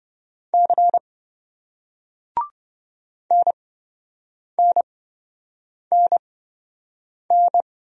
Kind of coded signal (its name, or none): Morse code